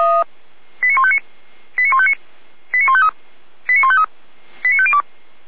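Five-tone selective call (ZVEI), heard as quick runs of five tones, tones of about 70 ms. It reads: 89159 89159 89153 89153 89502
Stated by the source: ZVEI five-tone call received by SDR, archived by Feuerwehr Orlamünde.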